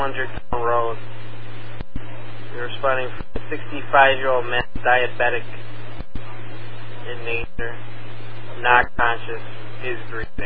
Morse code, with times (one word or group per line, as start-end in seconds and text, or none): none